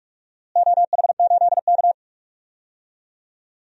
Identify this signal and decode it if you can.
Morse code, heard as a keyed tone, about 47 wpm, 700 Hz. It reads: OH8K
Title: DXCC-100 Call Signs - 4 Characters - Repeated 3x - 45wpm